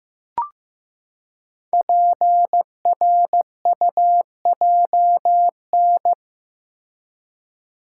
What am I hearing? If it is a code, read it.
Morse code, heard as a keyed tone, about 15 wpm, 700 Hz. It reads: PRUJN
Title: Code Groups - 5 Letters - 15wpm